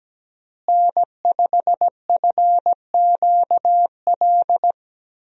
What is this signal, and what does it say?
Morse code, heard as a keyed tone, about 17 wpm, 700 Hz. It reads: N5FQL
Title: USA Call Signs - 5 Characters - Repeated 2x - 17wpm